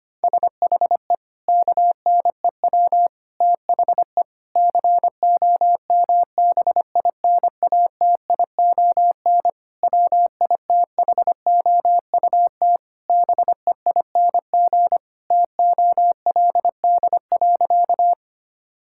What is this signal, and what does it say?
Morse code, heard as a keyed tone, about 25 wpm, 700 Hz. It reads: SHE KNEW THE COMBINATION WITHOUT BEING TOLD.
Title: Sentences from Top 1000 Words - Review Part 6 - 25wpm